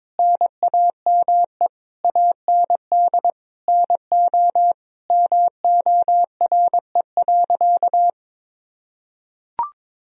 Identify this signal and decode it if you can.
Morse code, heard as a keyed tone, about 22 wpm, 700 Hz. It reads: NAME AND NO MORE.
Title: Sentences from Top 1000 Words - Review Part 1 - 22wpm